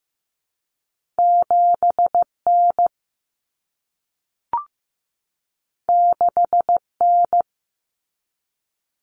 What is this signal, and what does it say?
Morse code, heard as a keyed tone, about 15 wpm, 700 Hz. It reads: 7N 6N
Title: Common Cut Numbers - 2-Digit 15wpm